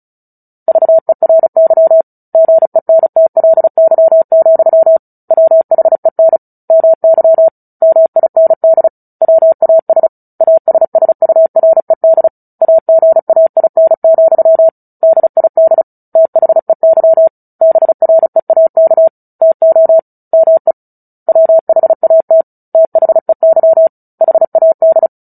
Morse code, heard as a keyed tone, about 34 wpm, 650 Hz. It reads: VERY GENTLY, WHEN MY MIND WAS ASSURED AGAIN, DID THEY BREAK TO ME WHAT THEY HAD
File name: Book Two / 09-Chapter 9 - 35 WPM